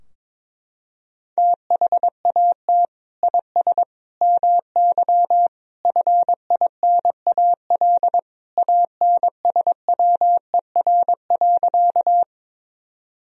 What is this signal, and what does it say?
Morse code, heard as a keyed tone, about 22 wpm, 700 Hz. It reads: THAT IS MY FINAL ANSWER.
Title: Sentences from Top 1000 Words - Review Part 2 - 22wpm